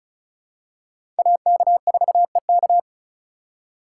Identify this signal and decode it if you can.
Morse code, heard as a keyed tone, about 36 wpm, 700 Hz. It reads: AK4EK
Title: USA Call Signs - 5 Characters - 4x Character Spacing - 35wpm